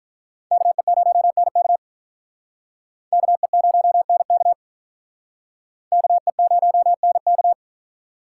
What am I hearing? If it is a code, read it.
Morse code, heard as a keyed tone, about 47 wpm, 700 Hz. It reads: KE0NK KE0NK KE0NK